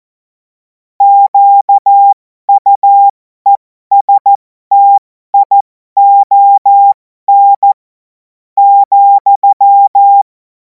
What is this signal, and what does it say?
Morse code, heard as a keyed tone, about 14 wpm, spaced as for 11 wpm, 800 Hz. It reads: QUESTION,
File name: Morsekurs / Witze